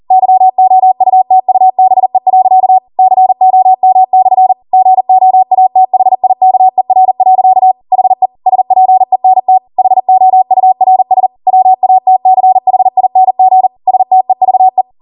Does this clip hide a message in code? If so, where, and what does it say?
Yes: Morse code from 0.1 to 14.8 s, heard as a keyed tone, about 40 wpm, 750 Hz. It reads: YOUTUBE. COM/ GOATHIKER. HE SPENT HOURS WATCHING STEVE